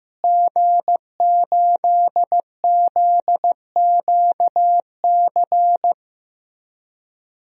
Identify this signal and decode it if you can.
Morse code, heard as a keyed tone, about 15 wpm, 700 Hz. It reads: G8ZQC
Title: DXCC-100 Call Signs - 5 Characters - Repeated 3x - 15wpm